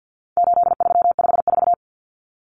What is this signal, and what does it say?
Morse code, heard as a keyed tone, about 50 wpm, 700 Hz. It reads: 8354
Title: Numbers - 4-Digit 50wpm